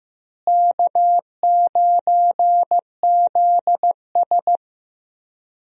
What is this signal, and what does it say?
Morse code, heard as a keyed tone, about 15 wpm, 700 Hz. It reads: K9ZS
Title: USA Call Signs - 4 Characters - Repeated 3x - 15wpm